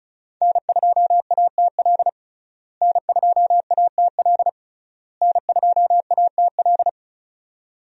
Morse code, heard as a keyed tone, about 36 wpm, 700 Hz. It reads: N2ATL N2ATL N2ATL
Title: USA Call Signs - 5 Characters - Repeated 3x - 35wpm